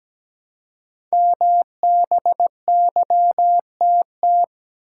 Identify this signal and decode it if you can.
Morse code, heard as a keyed tone, about 17 wpm, 700 Hz. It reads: MBYTT
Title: Code Groups - 5 Letter:Numbers - 17wpm